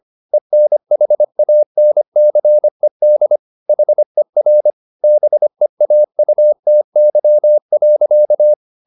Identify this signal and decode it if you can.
Morse code, heard as a keyed tone, about 25 wpm, 600 Hz. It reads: ENHANCED HER BEAUTY.